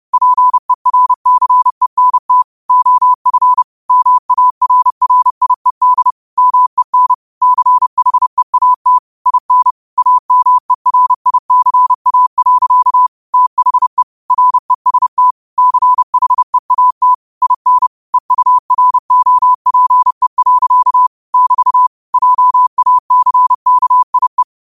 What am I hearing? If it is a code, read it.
Morse code, heard as a keyed tone, about 30 wpm, 1000 Hz. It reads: PERCENT OF MARRIED MEN CHEAT IN AMERICA. THE REST CHEAT IN EUROPE. <BT> JACKIE
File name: qotdfiles / 30wpm